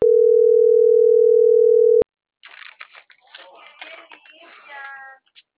Ringback tone (North America), 440 + 480 Hz.